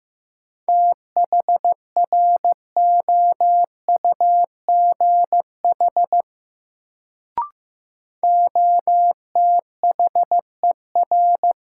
Morse code, heard as a keyed tone, about 15 wpm, 700 Hz. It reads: THROUGH OTHER
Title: Sets of 2 Words - New Word Drill - 15wpm